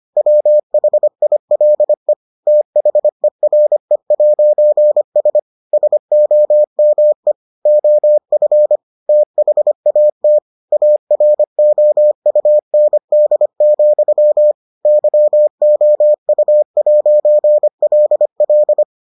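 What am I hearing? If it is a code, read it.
Morse code, heard as a keyed tone, about 25 wpm, 600 Hz. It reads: WHILE THERE'S SOME OF THAT AROUND, YOU'LL
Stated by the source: Morse edition of the Foundations of Amateur Radio episodes.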